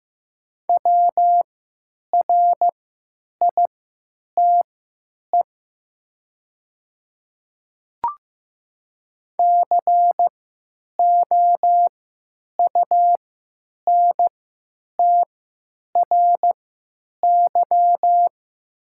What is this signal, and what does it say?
Morse code, heard as a keyed tone, about 15 wpm, spaced as for 8 wpm, 700 Hz. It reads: WRITE COUNTRY
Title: 3x Character Spacing - Top 500 Words - 15wpm